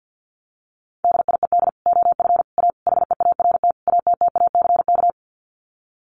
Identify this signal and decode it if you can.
Morse code, heard as a keyed tone, about 48 wpm, 700 Hz. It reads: DIED OF A HEART ATTACK
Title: Sets of 5 Words Encore2 50wpm